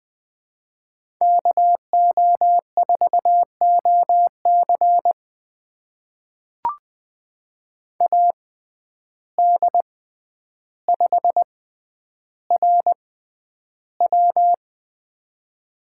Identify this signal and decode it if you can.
Morse code, heard as a keyed tone, about 20 wpm, spaced as for 10 wpm, 700 Hz. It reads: KO4OC AD5RW
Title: USA Call Signs - 6x Character Spacing - 20wpm